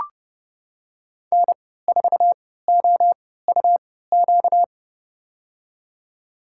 Morse code, heard as a keyed tone, about 30 wpm, spaced as for 17 wpm, 700 Hz. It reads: N4OUQ
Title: USA Call Signs - 5 Characters - 3x Character Spacing - 30wpm